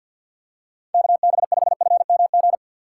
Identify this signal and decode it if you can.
Morse code, heard as a keyed tone, about 46 wpm, 690 Hz. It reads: KB5FMG
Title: USA Call Signs - 6 Characters - 3x Character Spacing - 50wpm